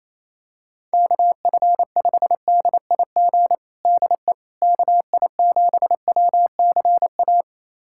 Morse code, heard as a keyed tone, about 28 wpm, 700 Hz. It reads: KF5DIG DE KI7WCA
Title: USA Callsign DE Callsign 28wpm